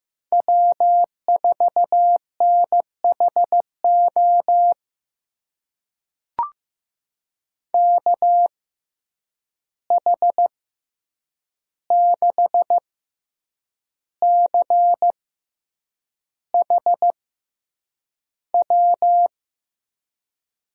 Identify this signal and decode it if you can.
Morse code, heard as a keyed tone, about 16 wpm, spaced as for 7 wpm, 700 Hz. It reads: W4NHO KH6CHW
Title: USA Call Signs - 6x Character Spacing - 15wpm